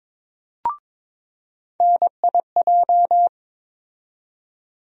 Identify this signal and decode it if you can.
Morse code, heard as a keyed tone, about 22 wpm, 700 Hz. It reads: NIJ
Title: USA Call Sign Postfixes - 3 Letters 22wpm